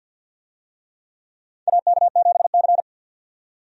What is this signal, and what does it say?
Morse code, heard as a keyed tone, about 51 wpm, 700 Hz. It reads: AK7C
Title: USA Call Signs - 4 Characters - 50wpm